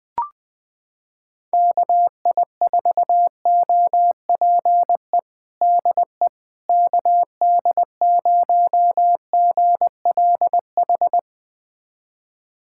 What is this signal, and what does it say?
Morse code, heard as a keyed tone, about 20 wpm, 700 Hz. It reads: KI4OPE DE KD0GLH
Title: USA Callsign DE Callsign 20wpm